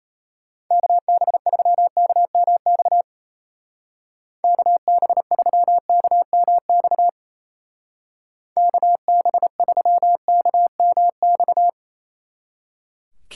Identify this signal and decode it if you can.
Morse code, heard as a keyed tone, about 35 wpm, 700 Hz. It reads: KB3KMX KB3KMX KB3KMX